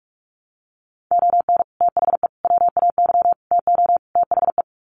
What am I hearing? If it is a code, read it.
Morse code, heard as a keyed tone, about 44 wpm, 700 Hz. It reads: ON THE WAY TO THE
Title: Sets of 5 Words 45wpm